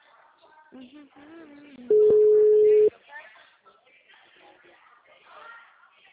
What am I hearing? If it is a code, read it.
Ringback tone (Europe, 425 Hz), with speech.